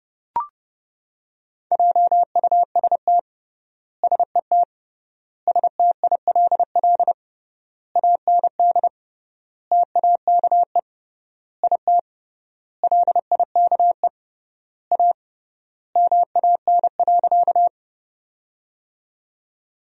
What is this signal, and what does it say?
Morse code, heard as a keyed tone, about 32 wpm, 700 Hz. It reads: JUST SET STILL AND TAKE IT LIKE A MAN.
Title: 3x Word-Spacing - Sentences from Top 300 Words - 30wpm